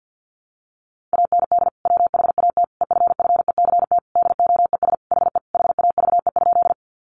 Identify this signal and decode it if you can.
Morse code, heard as a keyed tone, about 50 wpm, 700 Hz. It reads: AND WHAT EFFECT DOES HE HAVE?
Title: Sentences from Top 1000 Words 50wpm